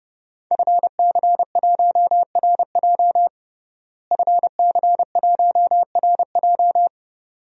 Morse code, heard as a keyed tone, about 32 wpm, 700 Hz. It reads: FC1RJ FC1RJ
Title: DXCC-100 Call Signs - 5 Characters - Repeated 2x - 30wpm